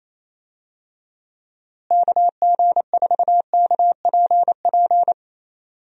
Morse code, heard as a keyed tone, about 27 wpm, 700 Hz. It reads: KG4KPP
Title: USA Call Signs - 6 Characters - Repeated 3x - 28wpm